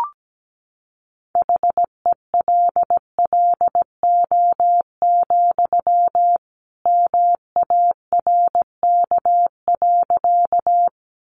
Morse code, heard as a keyed tone, about 17 wpm, 700 Hz. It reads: HELLO, MARK.
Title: Sentences from Top 1100 Words 17wpm